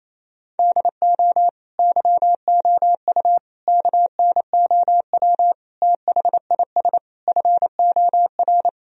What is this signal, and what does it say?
Morse code, heard as a keyed tone, about 28 wpm, 700 Hz. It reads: DO YOU KNOW THIS FOR
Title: Sentences from Top 1000 Words - Review Part 2 - 28wpm